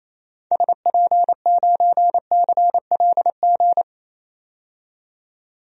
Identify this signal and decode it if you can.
Morse code, heard as a keyed tone, about 28 wpm, 700 Hz. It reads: SP9CLG
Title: DXCC-100 Call Signs - 6 Characters - 28wpm